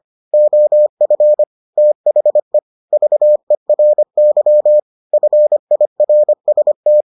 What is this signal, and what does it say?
Morse code, heard as a keyed tone, about 25 wpm, 600 Hz. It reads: OF THE VERY FIRST